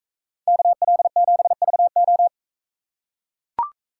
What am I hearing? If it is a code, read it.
Morse code, heard as a keyed tone, about 40 wpm, 700 Hz. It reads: KL7VO